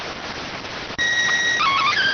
SSTV (slow-scan television), PD 290.